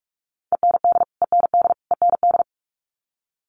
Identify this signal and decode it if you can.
Morse code, heard as a keyed tone, about 44 wpm, 700 Hz. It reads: END END END
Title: Top 500 Words Repeated 45wpm